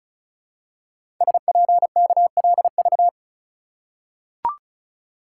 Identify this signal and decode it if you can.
Morse code, heard as a keyed tone, about 36 wpm, 690 Hz. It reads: SPKLV
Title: Code Groups - 5 Letters - 35wpm